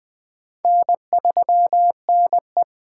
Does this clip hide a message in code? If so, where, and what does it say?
Yes: Morse code from 0.6 to 2.6 s, heard as a keyed tone, about 21 wpm, 700 Hz. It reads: N3NE